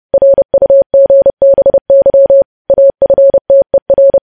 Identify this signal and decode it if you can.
Morse code, heard as a keyed tone, about 30 wpm, 550 Hz. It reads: RUGBY AFTER
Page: news headlines in morse code at 30 wpm